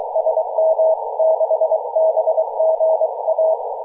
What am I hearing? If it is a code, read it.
Morse code, heard as a keyed tone, about 24 wpm, 690 Hz, with radio noise. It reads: SM6BGA
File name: posts / 2016-aug-20-0742z-kh0-w5sea-18-071mhz-cw